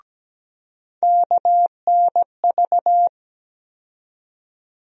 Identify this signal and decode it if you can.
Morse code, heard as a keyed tone, about 17 wpm, 700 Hz. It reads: KNV
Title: USA Call Sign Postfixes - 3 Letters 17wpm